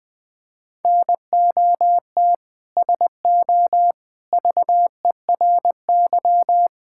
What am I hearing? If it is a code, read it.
Morse code, heard as a keyed tone, about 20 wpm, 700 Hz. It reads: NOT SO VERY
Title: Sentences from Top 1600 Words 20wpm